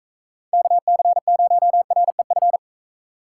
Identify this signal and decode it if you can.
Morse code, heard as a keyed tone, about 42 wpm, 700 Hz. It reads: KC0REF